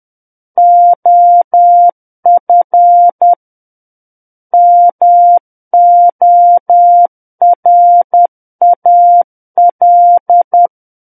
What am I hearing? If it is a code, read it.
Morse code, heard as a keyed tone, about 10 wpm, 700 Hz. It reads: OF MORAL
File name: Book One / 01-Chapter 1 - 10 WPM